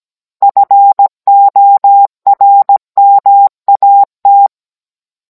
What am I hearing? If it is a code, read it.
Morse code, heard as a keyed tone, about 17 wpm, 800 Hz. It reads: FORMAT